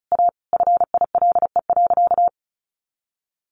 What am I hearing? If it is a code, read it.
Morse code, heard as a keyed tone, about 34 wpm, 700 Hz. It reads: A FILE.
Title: Sentences from Top 1200 Words 35wpm